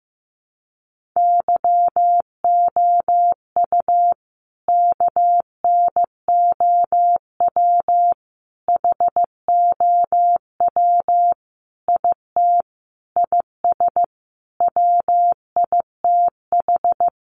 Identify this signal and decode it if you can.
Morse code, heard as a keyed tone, about 15 wpm, 700 Hz. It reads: YOU KNOW HOW IT IS WITH
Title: Sentences from Top 100 Words 15wpm